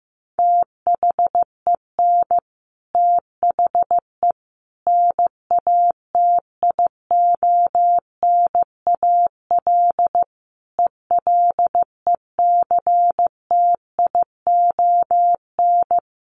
Morse code, heard as a keyed tone, about 14 wpm, 700 Hz. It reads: THEN THE NATIONAL ELECTION
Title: Sentences from Top 1500 Words 15wpm